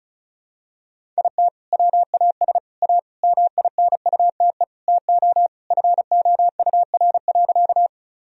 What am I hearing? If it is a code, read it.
Morse code, heard as a keyed tone, about 36 wpm, 700 Hz. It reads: IT WAS A MINUTE TO FOUR.
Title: Sentences from Top 1000 Words - Review Part 1 - 35wpm